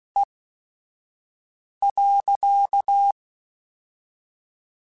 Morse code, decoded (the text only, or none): E.